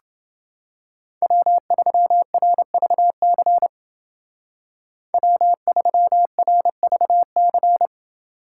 Morse code, heard as a keyed tone, about 29 wpm, 700 Hz. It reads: W3RVC W3RVC